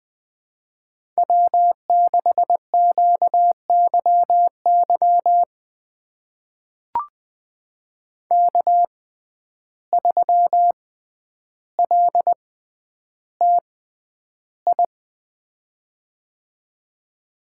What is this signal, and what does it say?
Morse code, heard as a keyed tone, about 20 wpm, spaced as for 10 wpm, 700 Hz. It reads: W6QYY K3LTI
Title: USA Call Signs - 5 Characters - 6x Character Spacing - 20wpm